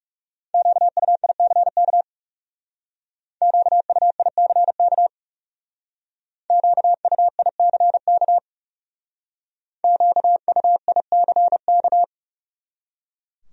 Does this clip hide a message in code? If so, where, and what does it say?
Yes: Morse code from 0.5 to 12.0 s, heard as a keyed tone, about 40 wpm, 700 Hz. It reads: QUICK QUICK QUICK QUICK